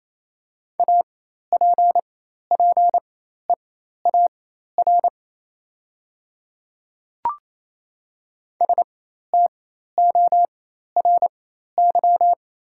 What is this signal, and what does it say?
Morse code, heard as a keyed tone, about 28 wpm, spaced as for 13 wpm, 700 Hz. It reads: APPEAR STORY
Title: 4x Character Spacing - Top 500 Words - 28wpm